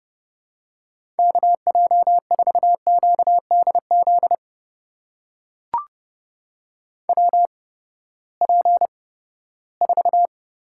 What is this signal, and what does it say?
Morse code, heard as a keyed tone, about 31 wpm, spaced as for 17 wpm, 700 Hz. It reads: KJ4QDZ WP4
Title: USA Call Signs - 6 Characters - 8x Character Spacing - 30wpm